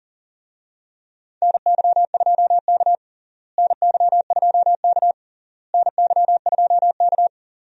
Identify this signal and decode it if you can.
Morse code, heard as a keyed tone, about 40 wpm, 700 Hz. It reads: NY2K NY2K NY2K
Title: USA Call Signs - 4 Characters - Repeated 3x - 40wpm